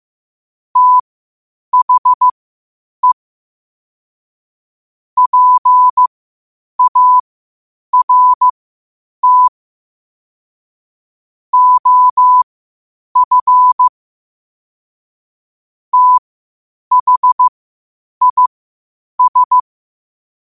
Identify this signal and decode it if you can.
Morse code, heard as a keyed tone, about 15 wpm, spaced as for 8 wpm, 1000 Hz. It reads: THE PART OF THIS
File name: qotdfiles / 8wpm